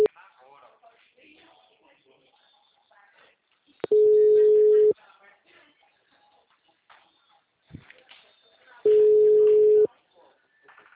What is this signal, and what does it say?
Ringback tone (Europe, 425 Hz), with speech.